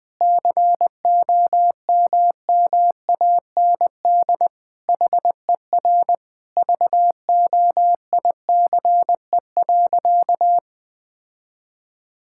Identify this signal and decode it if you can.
Morse code, heard as a keyed tone, about 20 wpm, 700 Hz. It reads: COMMAND HER VOICE.
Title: Sentences from Top 1600 Words 20wpm